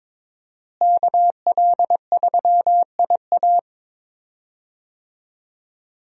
Morse code, heard as a keyed tone, about 22 wpm, 700 Hz. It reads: KL3IA
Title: USA Call Signs - 5 Characters - Repeated 3x - 22wpm